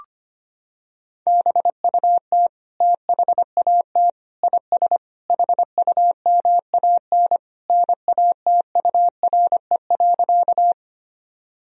Morse code, heard as a keyed tone, about 25 wpm, 700 Hz. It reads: BUT THAT IS HUMAN NATURE.
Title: Sentences from Top 1000 Words - Review Part 3 - 25wpm